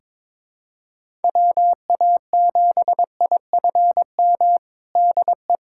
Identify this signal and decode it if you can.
Morse code, heard as a keyed tone, about 22 wpm, 700 Hz. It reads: WA7IFM DE
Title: USA Callsign DE Callsign - 6 Characters - 22wpm